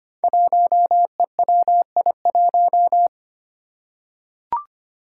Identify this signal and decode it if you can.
Morse code, heard as a keyed tone, about 25 wpm, 700 Hz. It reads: 1EWI1
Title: Five Challenging Letter-Number Combinations 25wpm